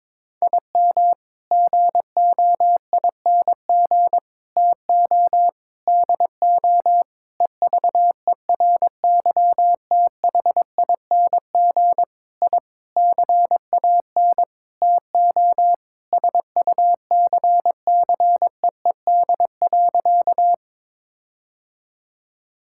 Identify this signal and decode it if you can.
Morse code, heard as a keyed tone, about 22 wpm, 700 Hz. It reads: IM GOING TO DO EVERYTHING I CAN TO SUCCEED.
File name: practice / 22wpm / Sentences with 8-Word Phrases - 22wpm